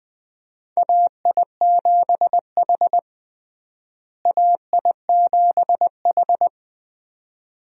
Morse code, heard as a keyed tone, about 20 wpm, 700 Hz. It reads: AI7H AI7H
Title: USA Call Signs - 4 Characters - Repeated 2x - 20wpm